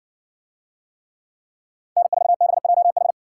Morse code, heard as a keyed tone, about 56 wpm, 700 Hz. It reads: N4BJH